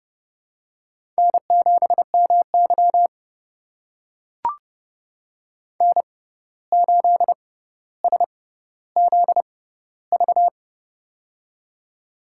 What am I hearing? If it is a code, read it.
Morse code, heard as a keyed tone, about 30 wpm, spaced as for 14 wpm, 700 Hz. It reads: N7MY N8SZV
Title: USA Call Signs - 6x Character Spacing - 30wpm